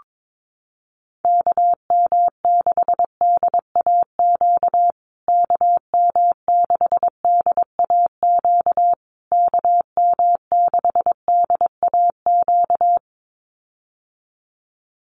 Morse code, heard as a keyed tone, about 22 wpm, 700 Hz. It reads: KM6DAQ KM6DAQ KM6DAQ
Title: USA Call Signs Repeated 3x 22wpm